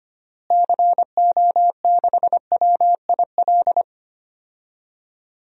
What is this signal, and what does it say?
Morse code, heard as a keyed tone, about 25 wpm, 700 Hz. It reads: CO6WIL
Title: DXCC-100 Call Signs - 6 Characters - Repeated 2x - 25wpm